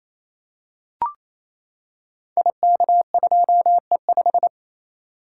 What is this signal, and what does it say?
Morse code, heard as a keyed tone, about 27 wpm, 700 Hz. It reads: IK2E5
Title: Code Groups - 5 Letter:Numbers - 28wpm